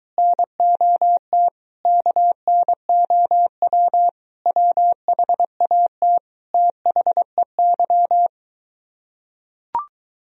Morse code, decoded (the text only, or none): NOT KNOW WHAT THEY